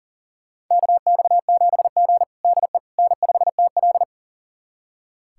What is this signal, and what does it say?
Morse code, heard as a keyed tone, about 40 wpm, 700 Hz. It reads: KX7G DE N5TL
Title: USA Callsign DE Callsign - 4 Characters - 40wpm